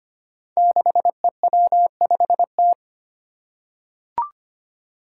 Morse code, heard as a keyed tone, about 25 wpm, 700 Hz. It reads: 6EW5T
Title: Five Challenging Letter-Number Combinations 25wpm